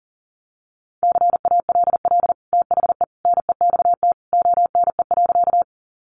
Morse code, heard as a keyed tone, about 40 wpm, 700 Hz. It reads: CALL THE NEXT ONE.
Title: Sentences from Top 300 Words 40wpm